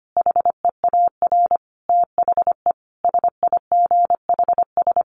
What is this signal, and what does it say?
Morse code, heard as a keyed tone, about 25 wpm, 700 Hz. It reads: HEAR THE SIGHS
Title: A Sound Of Thunder 25wpm